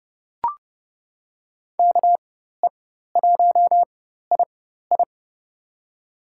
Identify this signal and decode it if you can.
Morse code, heard as a keyed tone, about 31 wpm, spaced as for 14 wpm, 700 Hz. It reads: KE1II
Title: USA Call Signs - 5 Characters - 4x Character Spacing - 30wpm